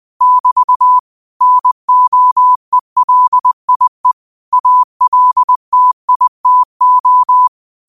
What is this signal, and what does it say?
Morse code, heard as a keyed tone, about 20 wpm, 1000 Hz. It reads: <BT> NOELIE ALTITO
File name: qotdfiles / 20wpm